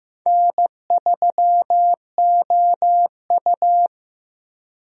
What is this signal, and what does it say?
Morse code, heard as a keyed tone, about 15 wpm, 700 Hz. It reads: N3OU
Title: USA Call Signs - 4 Characters - 15wpm